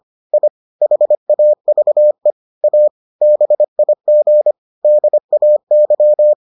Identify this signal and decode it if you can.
Morse code, heard as a keyed tone, about 25 wpm, 600 Hz. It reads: I HAVE A BIG DAY